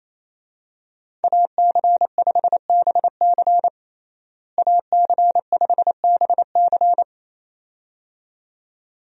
Morse code, heard as a keyed tone, about 28 wpm, 700 Hz. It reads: AC5BC AC5BC
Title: USA Call Signs - 5 Characters - Repeated 2x - 28wpm